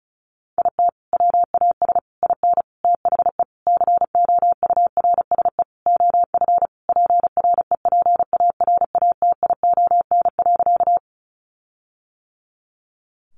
Morse code, heard as a keyed tone, about 34 wpm, 700 Hz. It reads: IT WAS IN THE COURSE OF PREPARATION.
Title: Sentences from Top 1400 Words 35wpm